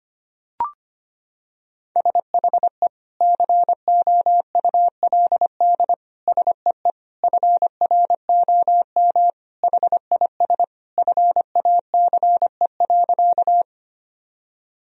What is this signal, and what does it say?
Morse code, heard as a keyed tone, about 25 wpm, 700 Hz. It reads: SHE COULD SEE FROM HIS FACE.
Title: Sentences from Top 1000 Words - Review Part 2 - 25wpm